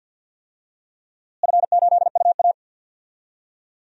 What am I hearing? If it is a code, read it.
Morse code, heard as a keyed tone, about 47 wpm, 700 Hz. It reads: F8UA
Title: DXCC-100 Call Signs - 4 Characters - Repeated 3x - 50wpm